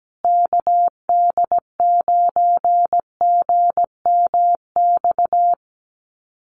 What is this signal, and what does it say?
Morse code, heard as a keyed tone, about 17 wpm, 700 Hz. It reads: KD9GMX
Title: USA Call Signs Repeated 3x 17wpm